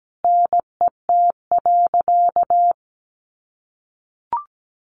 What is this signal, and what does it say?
Morse code, heard as a keyed tone, about 17 wpm, 700 Hz. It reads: NET.
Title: Sentences from Top 1500 Words 17wpm